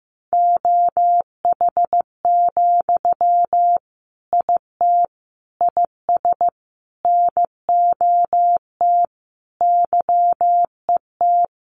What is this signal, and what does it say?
Morse code, heard as a keyed tone, about 15 wpm, 700 Hz. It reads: OH, IT IS NOT YET
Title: Sentences from Top 1100 Words 15wpm